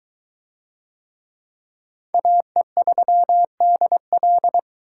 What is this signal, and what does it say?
Morse code, heard as a keyed tone, about 22 wpm, 700 Hz. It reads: AE3DL